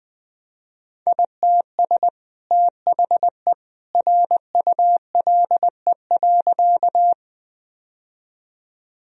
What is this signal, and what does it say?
Morse code, heard as a keyed tone, about 19 wpm, 700 Hz. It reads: ITS THE RULE.
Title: Sentences from Top 1000 Words - Review Part 2 - 20wpm